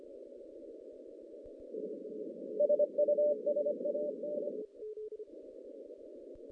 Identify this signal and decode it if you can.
Morse code, heard as a keyed tone, about 25 wpm, 590 Hz. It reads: SUSAN
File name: 2016 JAN CW NAQP / 2016 Jan naqp k5ot - 0-10